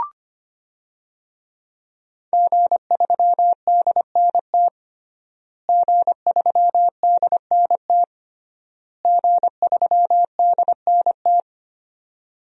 Morse code, heard as a keyed tone, about 25 wpm, 700 Hz. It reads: G3DNT G3DNT G3DNT
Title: DXCC-100 Call Signs - 5 Characters - Repeated 3x - 25wpm